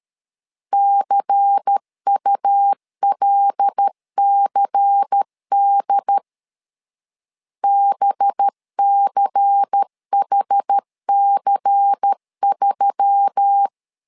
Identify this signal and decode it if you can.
Morse code, heard as a keyed tone, about 13 wpm, 800 Hz. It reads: CULCD BCHC3
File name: LEZIONI / 5 LBVC / 5 LBVC 12wpm 1955